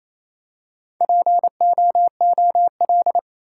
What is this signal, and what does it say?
Morse code, heard as a keyed tone, about 28 wpm, 700 Hz. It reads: POOL
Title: Four Letter Words - 28wpm